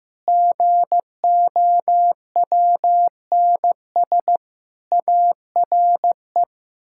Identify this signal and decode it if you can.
Morse code, heard as a keyed tone, about 15 wpm, 700 Hz. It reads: GOWNS ARE
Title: Pangram Challenge v2 - 15wpm